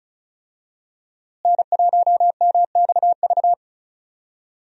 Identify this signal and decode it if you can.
Morse code, heard as a keyed tone, about 36 wpm, 700 Hz. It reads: N1MXV
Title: USA Call Signs - 5 Characters - Repeated 3x - 35wpm